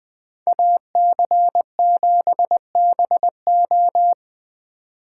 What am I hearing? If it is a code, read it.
Morse code, heard as a keyed tone, about 20 wpm, 700 Hz. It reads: AC7BO
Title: Speed Racing - USA 2x2 Call Signs 17wpm